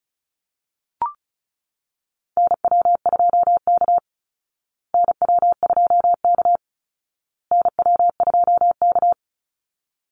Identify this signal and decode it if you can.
Morse code, heard as a keyed tone, about 34 wpm, 700 Hz. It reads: NW2K NW2K NW2K
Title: Call Signs Repeated 2x - 35wpm